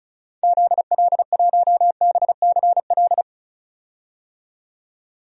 Morse code, heard as a keyed tone, about 34 wpm, 700 Hz. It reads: ZL1BCL
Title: DXCC-100 Call Signs - 6 Characters - 35wpm